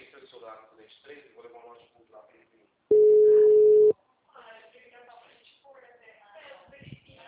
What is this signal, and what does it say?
Ringback tone (Europe, 425 Hz), with speech.
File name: out- -2017- - - .1357